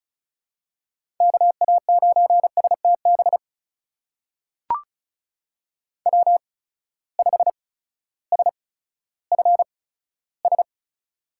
Morse code, heard as a keyed tone, about 36 wpm, spaced as for 16 wpm, 700 Hz. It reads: KA9STB W5SFS
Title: USA Call Signs - 8x Character Spacing - 35wpm